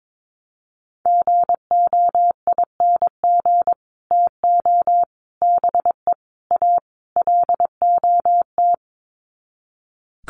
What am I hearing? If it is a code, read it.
Morse code, heard as a keyed tone, about 22 wpm, 700 Hz. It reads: GOING TO BE A LOT
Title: Sets of 5 Words Encore 22wpm